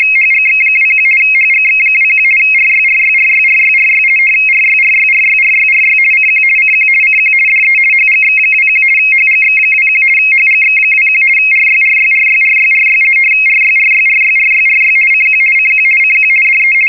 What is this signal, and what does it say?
RTTY, about 50 baud. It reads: HGX21 HGX21 RYRYRYRYRY RYRYRYRYRY CQ CQ CQ DE HGX21 HGX21 HGX21 RYRYRYRYRY RYRYRYRYRY CQ CQ CQ DE
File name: europe / hng / hng-hngd-mfa budapest-rtty-50 bd-(1986)-(rb)-(mc-914-1-16)